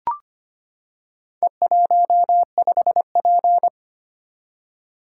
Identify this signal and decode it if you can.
Morse code, heard as a keyed tone, about 25 wpm, 700 Hz. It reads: E15P
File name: practice / 25wpm / Four Letter-Number Combinations - 25wpm